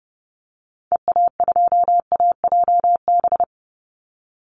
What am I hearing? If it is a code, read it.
Morse code, heard as a keyed tone, about 28 wpm, 700 Hz. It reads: EA2AJB
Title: Call Signs Repeated 2x - 30wpm